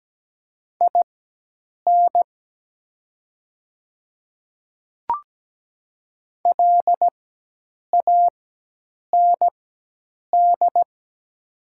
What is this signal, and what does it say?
Morse code, heard as a keyed tone, about 17 wpm, spaced as for 8 wpm, 700 Hz. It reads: IN LAND
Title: 4x Character Spacing - Top 200 Words - 17wpm